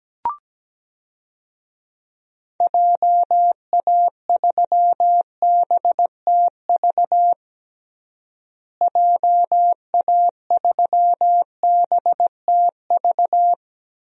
Morse code, heard as a keyed tone, about 17 wpm, 700 Hz. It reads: JA3BTV JA3BTV
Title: DXCC-100 Call Signs - 6 Characters - Repeated 3x - 17wpm